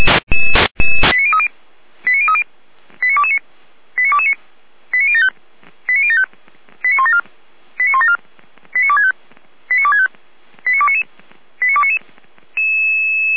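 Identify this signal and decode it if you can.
Five-tone selective call (ZVEI), heard as quick runs of five tones, tones of about 70 ms. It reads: 89030 89030 89229 89229 89075 89075 89174 89174 89265 89265 89299 89299
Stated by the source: ZVEI five-tone selective call 89108 received by SDR on channel path 455, logged by the Feuerwehr Orlamünde audio server.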